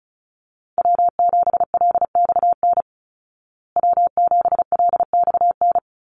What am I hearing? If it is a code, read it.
Morse code, heard as a keyed tone, about 34 wpm, 700 Hz. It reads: W7LXN W7LXN
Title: Call Signs Repeated 3x - 35wpm